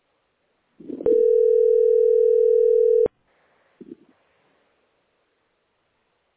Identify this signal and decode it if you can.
Ringback tone (North America), 440 + 480 Hz.